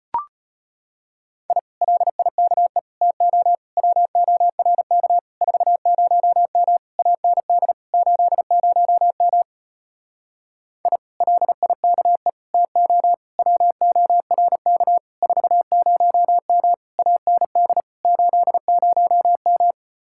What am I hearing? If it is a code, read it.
Morse code, heard as a keyed tone, about 33 wpm, 700 Hz. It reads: I LIKE TO WORK 40M AND 80M I LIKE TO WORK 40M AND 80M